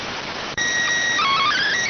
SSTV (slow-scan television), PD 290.